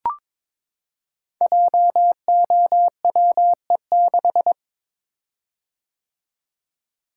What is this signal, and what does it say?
Morse code, heard as a keyed tone, about 22 wpm, 700 Hz. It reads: JOWE6
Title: Code Groups - 5 Letter:Numbers - 22wpm